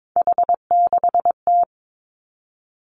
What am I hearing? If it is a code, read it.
Morse code, heard as a keyed tone, about 22 wpm, 700 Hz. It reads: H6T
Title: Three Challenging Letter-Number Combinations 22wpm